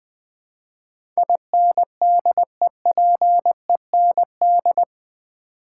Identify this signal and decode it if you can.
Morse code, heard as a keyed tone, about 20 wpm, 700 Hz. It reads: INDEPEND
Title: Build A Word - 4 Letters At A Time - 12-Letter Words - 20wpm